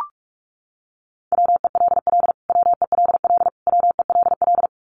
Morse code, heard as a keyed tone, about 44 wpm, 700 Hz. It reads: WELL WELL WELL
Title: Top 200 Words Repeated 45wpm